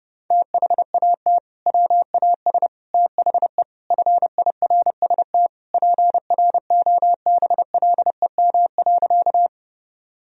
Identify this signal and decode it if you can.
Morse code, heard as a keyed tone, about 30 wpm, 700 Hz. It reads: THAT WAS THE FIRST PROBLEM.
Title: Sentences from Top 1000 Words - Review Part 2 - 30wpm